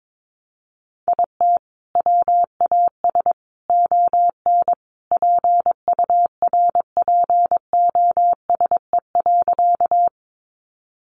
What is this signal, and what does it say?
Morse code, heard as a keyed tone, about 22 wpm, 700 Hz. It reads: IT WAS ON PURPOSE.
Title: Sentences from Top 800 Words 22wpm